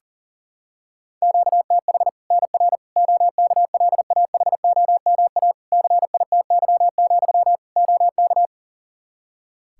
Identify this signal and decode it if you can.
Morse code, heard as a keyed tone, about 40 wpm, 700 Hz. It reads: QTH NR OKLAHOMA CITY, OK
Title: Practice QSO - Exchange 1 - Beginning - 40wpm